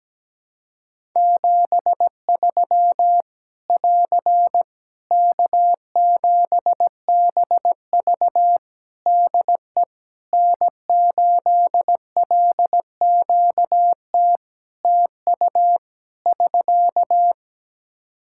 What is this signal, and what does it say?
Morse code, heard as a keyed tone, about 17 wpm, 700 Hz. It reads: 73 <AR> K7BV DE N8LQT TU <SK>
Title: Practice QSO - Exchange 4 - Ending - 17wpm